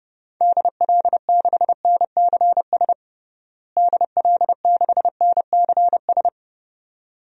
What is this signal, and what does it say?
Morse code, heard as a keyed tone, about 32 wpm, 700 Hz. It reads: DL6NCS DL6NCS
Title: DXCC-100 Call Signs - 6 Characters - Repeated 2x - 30wpm